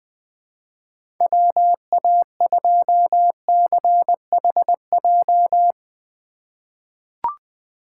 Morse code, heard as a keyed tone, about 20 wpm, 700 Hz. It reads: WA2CHJ
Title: USA Call Signs - 6 Characters - 3x Character Spacing - 20wpm